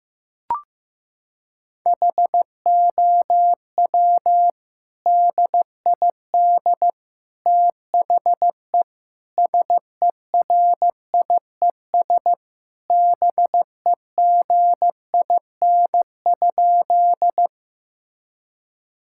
Morse code, heard as a keyed tone, about 15 wpm, 700 Hz. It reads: HOW DID THE SERIES BEGIN?
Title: Sentences from Top 1000 Words - Review Part 5 - 15wpm